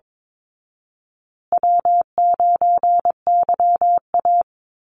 Morse code, heard as a keyed tone, about 22 wpm, 700 Hz. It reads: W9YA
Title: Call Signs Repeated 2x - 22wpm